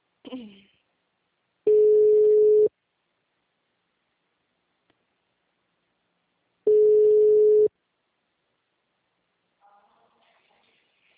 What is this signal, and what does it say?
Ringback tone (Europe, 425 Hz), with speech.
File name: out- -2012- - - .484